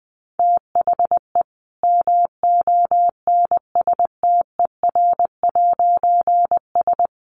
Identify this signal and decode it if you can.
Morse code, heard as a keyed tone, about 20 wpm, 700 Hz. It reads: THE MONSTER'S
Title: A Sound Of Thunder 20wpm